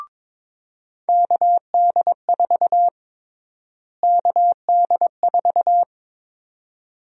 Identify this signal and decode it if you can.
Morse code, heard as a keyed tone, about 22 wpm, 700 Hz. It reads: KD4 KD4
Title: USA Call Sign Prefixes - Repeated 2x - 22wpm